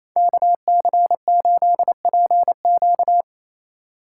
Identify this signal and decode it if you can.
Morse code, heard as a keyed tone, about 34 wpm, 700 Hz. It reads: KC8PQ
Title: Speed Racing - USA Call Signs 28wpm